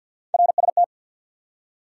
Morse code, heard as a keyed tone, about 50 wpm, 700 Hz. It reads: RST
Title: Top 90 QSO Elements 50wpm